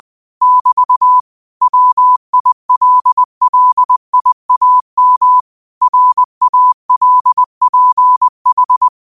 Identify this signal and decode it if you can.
Morse code, heard as a keyed tone, about 20 wpm, 1000 Hz. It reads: <BT> WILLIAM RALPH
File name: qotdfiles / 20wpm